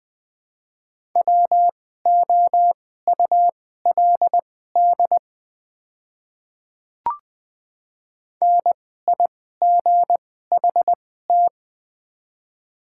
Morse code, heard as a keyed tone, about 21 wpm, spaced as for 15 wpm, 700 Hz. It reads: WOULD NIGHT
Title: 2x Character Spacing - Top 300 Words - 20wpm